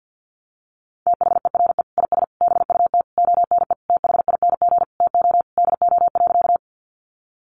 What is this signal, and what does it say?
Morse code, heard as a keyed tone, about 50 wpm, 700 Hz. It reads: THERE IS BUT ONE THING TO DO.
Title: Sentences from Top 100 Words 50wpm